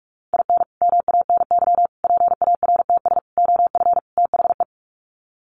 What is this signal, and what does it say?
Morse code, heard as a keyed tone, about 44 wpm, 700 Hz. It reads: IN MANY PARTS OF THE
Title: Sets of 5 Words Encore 45wpm